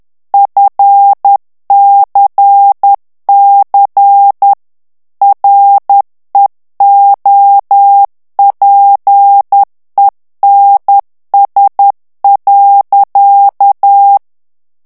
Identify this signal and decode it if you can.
Morse code, heard as a keyed tone, about 10 wpm, 800 Hz. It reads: FCC REOPENS.
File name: ANGLAIS texte clair 3 vitesse 10 mots minute